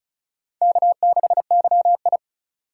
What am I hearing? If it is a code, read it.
Morse code, heard as a keyed tone, about 36 wpm, 700 Hz. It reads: K6YI
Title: USA Call Signs - 4 Characters - 3x Character Spacing - 35wpm